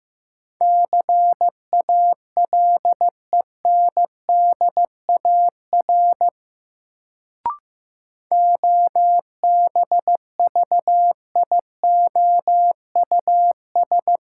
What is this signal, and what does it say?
Morse code, heard as a keyed tone, about 15 wpm, 700 Hz. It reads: CALENDAR OBVIOUS
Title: Sentences from Top 1600 Words - New Word Drill Alt - 15wpm